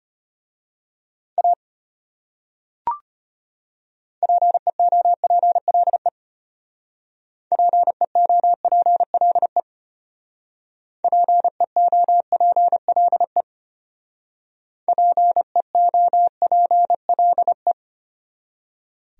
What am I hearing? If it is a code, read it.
Morse code, heard as a keyed tone, about 30 wpm, 700 Hz. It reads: A PEOPLE PEOPLE PEOPLE PEOPLE
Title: Speed-Racing Top 100 Words 25wpm